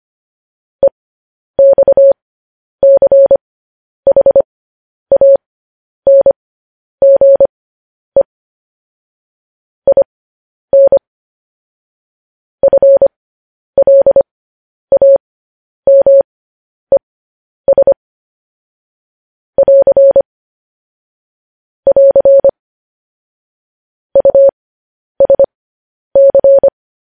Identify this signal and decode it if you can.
Morse code, heard as a keyed tone, about 25 wpm, spaced as for 10 wpm, 550 Hz. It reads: EXCHANGE IN FLAMES <AR> <AR> USC
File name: podcast / News headlines-1x-25x10